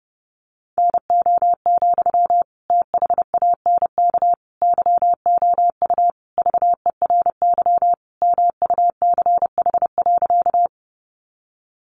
Morse code, heard as a keyed tone, about 30 wpm, 700 Hz. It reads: NO, THANK YOU VERY MUCH.
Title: Sentences from Top 600 Words 30wpm